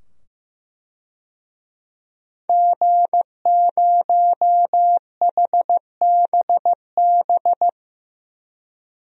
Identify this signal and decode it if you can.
Morse code, heard as a keyed tone, about 15 wpm, 700 Hz. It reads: G0HBB
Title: DXCC-100 Call Signs - 5 Characters - 15wpm